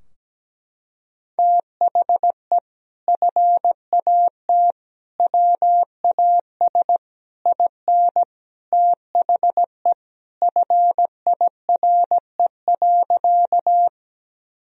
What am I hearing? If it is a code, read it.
Morse code, heard as a keyed tone, about 17 wpm, 700 Hz. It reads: THE FAT WAS IN THE FIRE.
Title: Sentences from Top 1000 Words - Review Part 4 - 17wpm